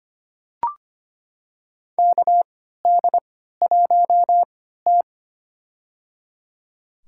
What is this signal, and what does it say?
Morse code, heard as a keyed tone, about 25 wpm, spaced as for 14 wpm, 700 Hz. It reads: KD1T
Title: USA Call Signs - 4 Characters - 3x Character Spacing - 25wpm